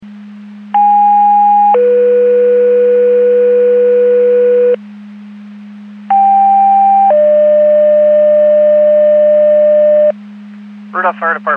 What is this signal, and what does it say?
Two-tone page, heard as a one-second tone, then a three-second tone, about 820 then 490 Hz, about 790 then 600 Hz, with speech. It is heard from 0.7 to 10.1 s.